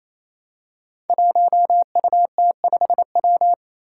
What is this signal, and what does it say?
Morse code, heard as a keyed tone, about 28 wpm, 700 Hz. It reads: 1UT5W
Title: Five Challenging Letter-Number Combinations 28wpm